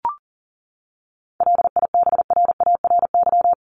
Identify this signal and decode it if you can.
Morse code, heard as a keyed tone, about 40 wpm, 700 Hz. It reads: LIBRARY
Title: Top 501-750 Words 40wpm